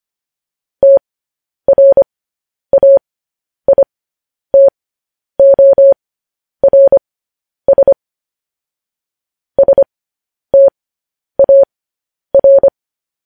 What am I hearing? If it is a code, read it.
Morse code, heard as a keyed tone, about 25 wpm, spaced as for 10 wpm, 550 Hz. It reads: TRAITORS STAR
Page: news-headlines-in-morse-code-at-10-wpm